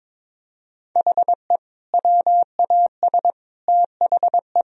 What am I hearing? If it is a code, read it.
Morse code, heard as a keyed tone, about 22 wpm, 700 Hz. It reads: HE WAS THE
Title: Sentences from Top 1000 Words - Review Part 1 - 22wpm